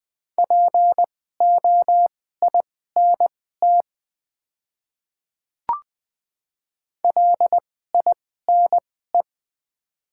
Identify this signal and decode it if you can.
Morse code, heard as a keyed tone, about 20 wpm, spaced as for 14 wpm, 700 Hz. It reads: POINT LINE
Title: 2x Character Spacing - Top 200 Words - 20wpm